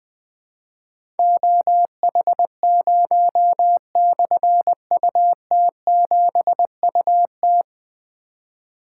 Morse code, heard as a keyed tone, about 20 wpm, 700 Hz. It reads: OH0/UT7UT
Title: Difficult Call Signs 20wpm